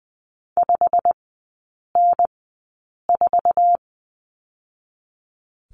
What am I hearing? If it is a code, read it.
Morse code, heard as a keyed tone, about 20 wpm, 700 Hz. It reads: 5 N 4
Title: 100 Lesson 17e - Letter u - Three Characters 20wpm